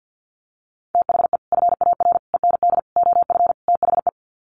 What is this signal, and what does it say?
Morse code, heard as a keyed tone, about 50 wpm, 700 Hz. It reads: THE FAR END OF THE
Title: Sets of 5 Words 50wpm